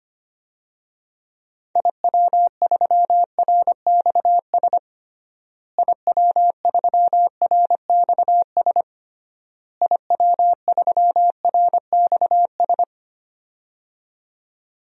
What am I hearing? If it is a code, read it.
Morse code, heard as a keyed tone, about 25 wpm, 700 Hz. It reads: IW3RXS IW3RXS IW3RXS
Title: DXCC-100 Call Signs - 6 Characters - Repeated 3x - 25wpm